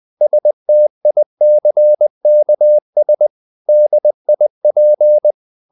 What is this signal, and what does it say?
Morse code, heard as a keyed tone, about 20 wpm, 600 Hz. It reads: STICKS DIP
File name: the old man and the sea / the old man and the sea part 1 20wpm